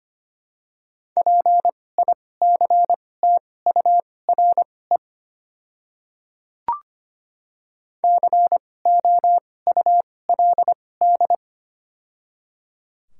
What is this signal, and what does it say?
Morse code, heard as a keyed tone, about 25 wpm, spaced as for 18 wpm, 700 Hz. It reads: PICTURE COULD
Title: 2x Character Spacing - Top 200 Words - 25wpm